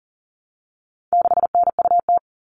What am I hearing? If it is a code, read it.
Morse code, heard as a keyed tone, about 40 wpm, 700 Hz. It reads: 6NUT
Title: All Cut Numbers - 4-Digit 40wpm